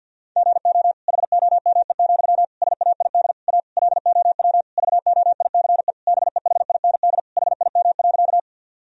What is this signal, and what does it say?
Morse code, heard as a keyed tone, about 52 wpm, 700 Hz. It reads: GO HOME, SAID A LOW VOICE BEHIND HIM.
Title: Sentences from Top 1000 Words - Review Part 1 - 50wpm